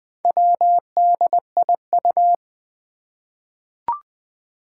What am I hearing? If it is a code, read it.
Morse code, heard as a keyed tone, about 20 wpm, 700 Hz. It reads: WDIU
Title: Four Challenging Letter-Number Combinations 20wpm